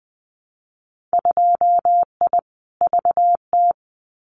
Morse code, heard as a keyed tone, about 20 wpm, 700 Hz. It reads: 2I VT
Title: Winter Field Day Exchanges 20wpm